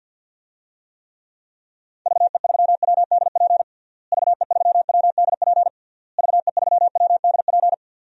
Morse code, heard as a keyed tone, about 51 wpm, 700 Hz. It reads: VE3WDP VE3WDP VE3WDP
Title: DXCC-100 Call Signs - 6 Characters - Repeated 3x - 50wpm